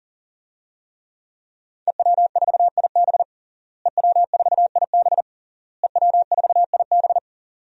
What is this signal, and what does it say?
Morse code, heard as a keyed tone, about 40 wpm, 700 Hz. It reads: EW4IB EW4IB EW4IB
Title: DXCC-100 Call Signs - 5 Characters - Repeated 3x - 40wpm